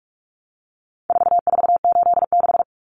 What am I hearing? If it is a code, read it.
Morse code, heard as a keyed tone, about 44 wpm, 700 Hz. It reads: 4486
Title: Numbers - 4-Digit 45wpm